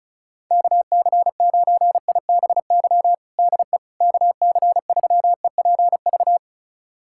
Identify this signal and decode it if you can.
Morse code, heard as a keyed tone, about 34 wpm, 700 Hz. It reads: KC9IBY DE KC3EPV
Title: USA Callsign DE Callsign - 6 Characters - 35wpm